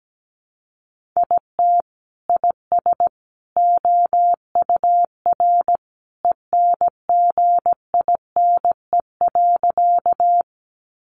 Morse code, heard as a keyed tone, about 17 wpm, 700 Hz. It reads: IT IS OUR ENGINE.
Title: Sentences from Top 1500 Words 17wpm